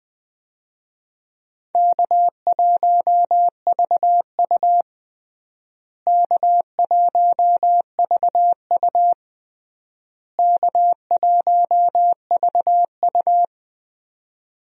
Morse code, heard as a keyed tone, about 20 wpm, 700 Hz. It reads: K1VU K1VU K1VU
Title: USA Call Signs - 4 Characters - Repeated 3x - 20wpm